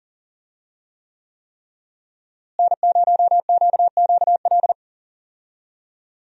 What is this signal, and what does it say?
Morse code, heard as a keyed tone, about 40 wpm, 700 Hz. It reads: N0QQL
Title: USA Call Signs - 5 Characters - Repeated 3x - 40wpm